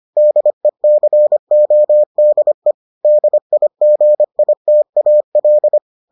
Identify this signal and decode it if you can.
Morse code, heard as a keyed tone, about 25 wpm, 600 Hz. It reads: DECODE DIGITAL